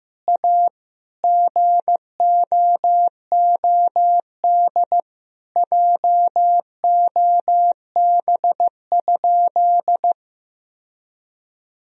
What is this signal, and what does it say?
Morse code, heard as a keyed tone, about 15 wpm, 700 Hz. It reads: A GOOD JOB?
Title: Sentences from Top 1000 Words - Review Part 3 - 15wpm